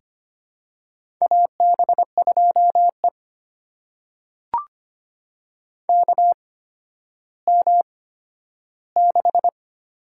Morse code, heard as a keyed tone, about 25 wpm, spaced as for 12 wpm, 700 Hz. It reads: AB2E KM6
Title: USA Call Signs - 8x Character Spacing - 25wpm